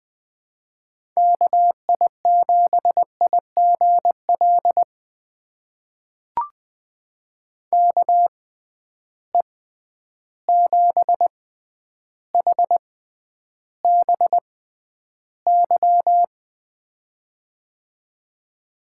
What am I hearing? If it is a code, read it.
Morse code, heard as a keyed tone, about 20 wpm, spaced as for 10 wpm, 700 Hz. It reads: KI7IGL KE7HBY
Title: USA Call Signs - 6 Characters - 6x Character Spacing - 20wpm